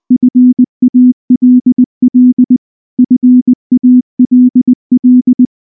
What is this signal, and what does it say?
Morse code, heard as a keyed tone, about 19 wpm, 250 Hz. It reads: FALL FALL
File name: morse / morse-training-20WPM-3